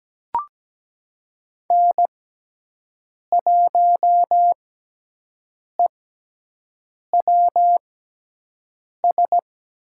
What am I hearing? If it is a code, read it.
Morse code, heard as a keyed tone, about 17 wpm, spaced as for 6 wpm, 700 Hz. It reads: N1EWS
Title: USA Call Signs - 5 Characters - 6x Character Spacing - 17wpm